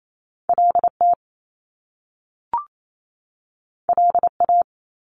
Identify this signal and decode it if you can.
Morse code, heard as a keyed tone, about 28 wpm, 700 Hz. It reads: L T L A
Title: Top Two Letter Combinations in Words 28wpm